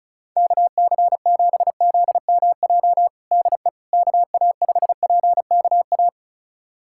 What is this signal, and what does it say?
Morse code, heard as a keyed tone, about 36 wpm, 700 Hz. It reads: KC7ZMJ DE KA5PKA
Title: USA Callsign DE Callsign - 6 Characters - 35wpm